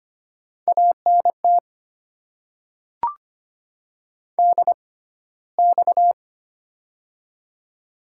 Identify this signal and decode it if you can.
Morse code, heard as a keyed tone, about 25 wpm, spaced as for 15 wpm, 700 Hz. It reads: ANT DX
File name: Top 90 QSO Elements - 6x Character Spacing - 25wpm